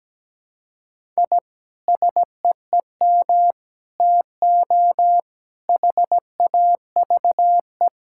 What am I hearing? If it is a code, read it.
Morse code, heard as a keyed tone, about 17 wpm, 700 Hz. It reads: I SEEM TO HAVE
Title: Sentences from Top 1000 Words - Review Part 5 - 17wpm